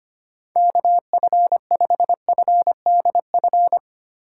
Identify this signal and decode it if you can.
Morse code, heard as a keyed tone, about 25 wpm, 700 Hz. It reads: KF5FDF
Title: USA Call Signs - 6 Characters - 3x Character Spacing - 25wpm